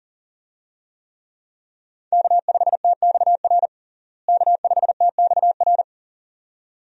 Morse code, heard as a keyed tone, about 40 wpm, 700 Hz. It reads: K5TXR K5TXR
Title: USA Call Signs - 5 Characters - Repeated 2x - 40wpm